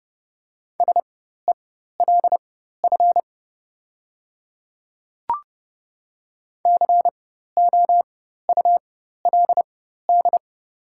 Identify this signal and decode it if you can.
Morse code, heard as a keyed tone, about 30 wpm, spaced as for 14 wpm, 700 Hz. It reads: SELF COULD
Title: 4x Character Spacing - Top 300 Words - 30wpm